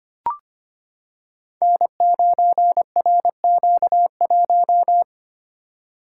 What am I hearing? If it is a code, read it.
Morse code, heard as a keyed tone, about 25 wpm, 700 Hz. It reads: N9RQ1
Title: Code Groups - 5 Letter:Numbers - 25wpm